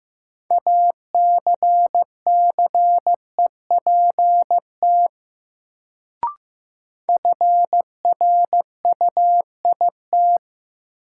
Morse code, heard as a keyed tone, about 15 wpm, 700 Hz. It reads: ACCEPT FRUIT
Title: Sentences from Top 1300 Words - New Word Drill Alt - 15wpm